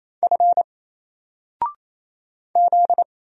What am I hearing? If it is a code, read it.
Morse code, heard as a keyed tone, about 27 wpm, 700 Hz. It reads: F Z
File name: Single-Letter-Number-28wpm